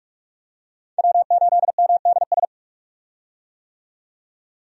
Morse code, heard as a keyed tone, about 46 wpm, 700 Hz. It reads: W8MDS
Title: USA Call Signs - 5 Characters - 45wpm